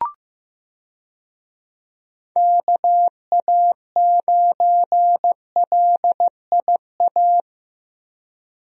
Morse code, heard as a keyed tone, about 15 wpm, 700 Hz. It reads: KA9LIA
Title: USA Call Signs - 6 Characters - Repeated 2x - 15wpm